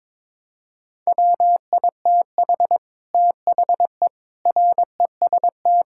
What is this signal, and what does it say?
Morse code, heard as a keyed tone, about 22 wpm, 700 Hz. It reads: WITH THE REST
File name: practice / 22wpm / Build A Phrase - 5-Word Phrases - 22wpm